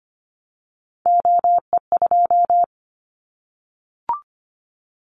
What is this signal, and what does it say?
Morse code, heard as a keyed tone, about 25 wpm, 700 Hz. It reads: OE2
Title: Callsign Prefixes for Top 25 DXCC - Repeated 3x - 25wpm